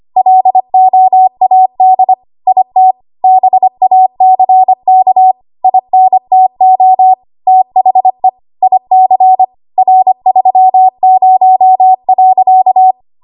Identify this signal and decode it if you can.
Morse code, heard as a keyed tone, about 25 wpm, 750 Hz. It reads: LOAD IT BACK INTO THE IC R30.
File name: Archive / 25 WPM / 25WPM